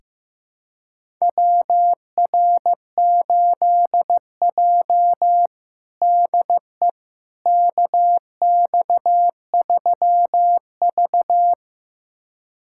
Morse code, heard as a keyed tone, about 15 wpm, 700 Hz. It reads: WR8J DE KX3V
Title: USA Callsign DE Callsign - 4 Characters - 15wpm